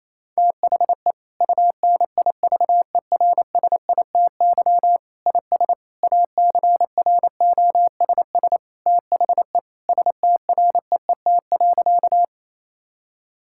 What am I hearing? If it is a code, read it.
Morse code, heard as a keyed tone, about 27 wpm, 700 Hz. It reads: THE UNIVERSITY IS ACROSS THE STREET.
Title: Sentences from Top 1000 Words - Review Part 5 - 28wpm